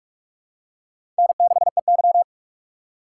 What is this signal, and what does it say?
Morse code, heard as a keyed tone, about 43 wpm, 700 Hz. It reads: N6EY
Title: USA Call Signs - 4x Character Spacing - 45wpm